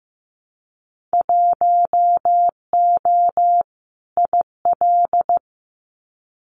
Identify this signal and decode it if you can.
Morse code, heard as a keyed tone, about 15 wpm, 700 Hz. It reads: 1O IL
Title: Winter Field Day Exchanges 15wpm